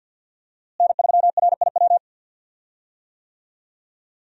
Morse code, heard as a keyed tone, about 53 wpm, 700 Hz. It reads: N3RIW
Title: USA Call Signs - 5 Characters - Repeated 3x - 50wpm